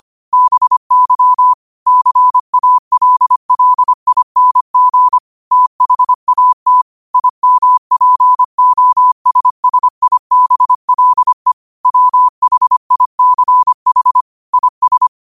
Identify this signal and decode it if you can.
Morse code, heard as a keyed tone, about 25 wpm, 1000 Hz. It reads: BY CALLING THAT IMPOSSIBLE WHICH IS
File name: qotdfiles / 25wpm